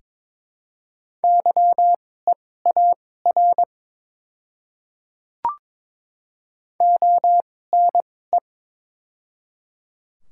Morse code, heard as a keyed tone, about 21 wpm, spaced as for 16 wpm, 700 Hz. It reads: YEAR ONE
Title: 2x Character Spacing - Top 200 Words - 22wpm